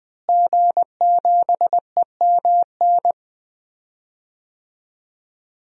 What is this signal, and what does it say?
Morse code, heard as a keyed tone, about 20 wpm, 700 Hz. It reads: G7EMN
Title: DXCC-100 Call Signs - 5 Characters - Repeated 3x - 20wpm